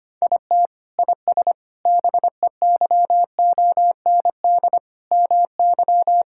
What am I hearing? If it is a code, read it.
Morse code, heard as a keyed tone, about 25 wpm, 700 Hz. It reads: IT IS BEYOND MY
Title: Sentences from Top 1000 Words - Review Part 6 - 25wpm